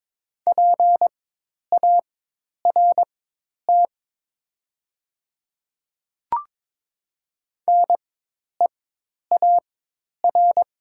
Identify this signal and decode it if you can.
Morse code, heard as a keyed tone, about 22 wpm, spaced as for 10 wpm, 690 Hz. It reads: PART NEAR
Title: 4x Character Spacing - Top 300 Words - 22wpm